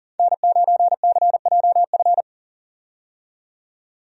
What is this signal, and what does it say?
Morse code, heard as a keyed tone, about 40 wpm, 700 Hz. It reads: N9CJF
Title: USA Call Signs - 5 Characters - 40wpm